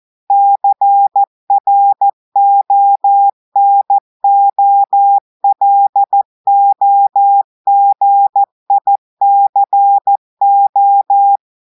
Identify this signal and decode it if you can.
Morse code, heard as a keyed tone, about 14 wpm, 800 Hz. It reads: CRONOLOGICO
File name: audios / portugais / portugais texte en clair 1 vitesse 14 mots minutes